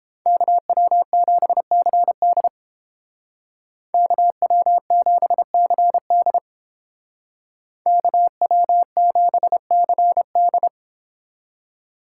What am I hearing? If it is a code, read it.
Morse code, heard as a keyed tone, about 30 wpm, 700 Hz. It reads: KW7CD KW7CD KW7CD